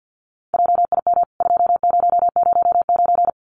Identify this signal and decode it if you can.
Morse code, heard as a keyed tone, about 50 wpm, 700 Hz. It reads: JIM 2009
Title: CWT Exchange 50wpm